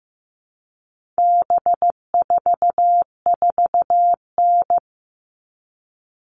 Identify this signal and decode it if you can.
Morse code, heard as a keyed tone, about 15 wpm, 700 Hz. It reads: B44N
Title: All Cut Numbers - 4-Digit 15wpm